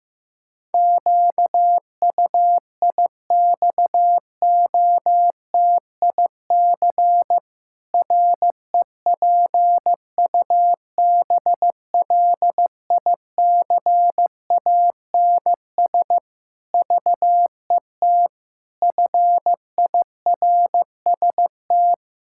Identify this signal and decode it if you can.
Morse code, heard as a keyed tone, about 15 wpm, 700 Hz. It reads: QUIXOTIC REPUBLICANS VET FIRST
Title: Pangram Challenge v2 - 15wpm